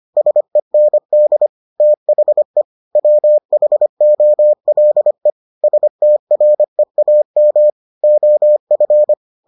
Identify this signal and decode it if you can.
Morse code, heard as a keyed tone, about 25 wpm, 600 Hz. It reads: SEND THE WHOLE STREAM OF